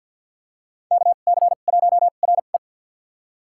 Morse code, heard as a keyed tone, about 53 wpm, spaced as for 37 wpm, 700 Hz. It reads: KC1RE
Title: USA Call Signs - 2x Character Spacing - 50wpm